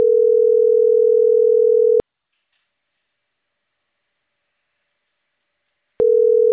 Ringback tone (North America), 440 + 480 Hz.
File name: CENTRAL CAEA / out agcaea01